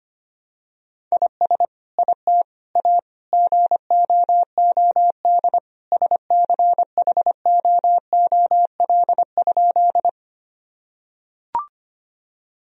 Morse code, heard as a keyed tone, about 25 wpm, 700 Hz. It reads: IS IT A GOOD SCHOOL?
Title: Sentences from Top 1000 Words - Review Part 1 - 25wpm